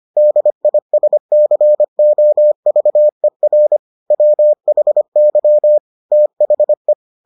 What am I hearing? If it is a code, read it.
Morse code, heard as a keyed tone, about 25 wpm, 600 Hz. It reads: DISCOVER WHY THE